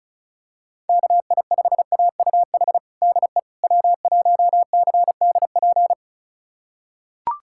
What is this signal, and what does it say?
Morse code, heard as a keyed tone, about 34 wpm, 700 Hz. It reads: KI5AUH DE W1CDP
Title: USA Callsign DE Callsign 35wpm